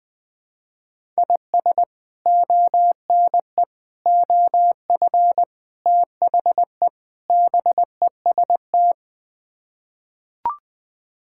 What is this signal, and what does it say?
Morse code, heard as a keyed tone, about 21 wpm, 700 Hz. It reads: IS ONE OF THE BEST
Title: Build A Phrase - 5-Word Phrases - 20wpm